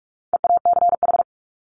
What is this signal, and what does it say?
Morse code, heard as a keyed tone, about 44 wpm, 700 Hz. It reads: EACH
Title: Top 200 Words 45wpm